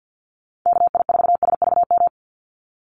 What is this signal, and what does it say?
Morse code, heard as a keyed tone, about 50 wpm, 700 Hz. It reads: KI4SVM
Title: NA SOTA Chaser Callsigns 50wpm